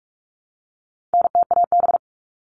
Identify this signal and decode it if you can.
Morse code, heard as a keyed tone, about 44 wpm, 700 Hz. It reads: NTAB
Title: All Cut Numbers - 4-Digit 45wpm